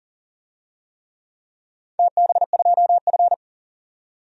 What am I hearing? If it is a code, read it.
Morse code, heard as a keyed tone, about 42 wpm, 700 Hz. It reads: TB2F
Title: DXCC-100 Call Signs - 4 Characters - 40wpm